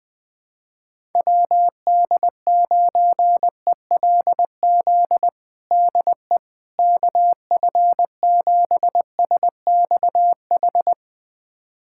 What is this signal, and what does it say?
Morse code, heard as a keyed tone, about 20 wpm, 700 Hz. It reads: WD9ELZ DE KF7SXH
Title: USA Callsign DE Callsign - 6 Characters - 20wpm